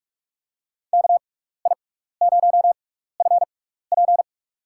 Morse code, heard as a keyed tone, about 44 wpm, spaced as for 15 wpm, 700 Hz. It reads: KI0FP
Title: USA Call Signs - 5 Characters - 6x Character Spacing - 45wpm